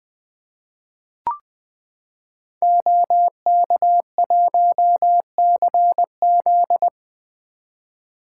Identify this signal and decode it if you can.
Morse code, heard as a keyed tone, about 20 wpm, 700 Hz. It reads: OK1CZ
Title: 2023 SOTA Activator Call Signs - 20wpm